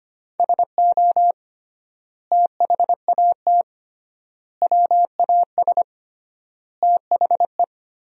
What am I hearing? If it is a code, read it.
Morse code, heard as a keyed tone, about 25 wpm, 700 Hz. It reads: SO THAT WAS THE
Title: 3x Word-Spacing - Sentences from Top 100 Words - 25wpm